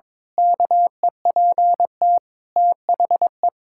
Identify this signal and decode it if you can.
Morse code, heard as a keyed tone, about 22 wpm, 700 Hz. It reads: KEPT THE
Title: Pangram Challenge v2 - 22wpm